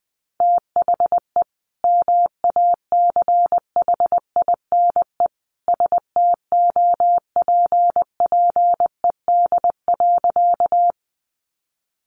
Morse code, heard as a keyed tone, about 20 wpm, 700 Hz. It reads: THE MACHINE STOPPED.
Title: A Sound Of Thunder 20wpm